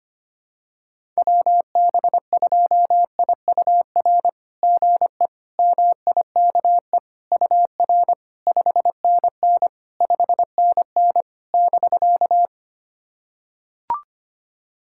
Morse code, heard as a keyed tone, about 25 wpm, 690 Hz. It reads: WB2IUR GE MIKE UR 5NN 5NN <BK>
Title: SOTA QSO Practice 25wpm